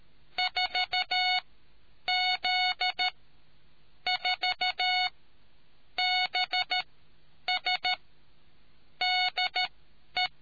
Morse code, heard as a keyed tone, about 13 wpm, spaced as for 8 wpm, 730 Hz. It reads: 4Z4BS DE